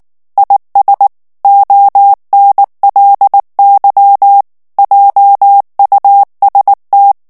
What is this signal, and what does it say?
Morse code, heard as a keyed tone, about 19 wpm, 800 Hz. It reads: IS ONLY JUST
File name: morse mp3 / ANGLAIS texte clair 2 vitesse 18 mots minute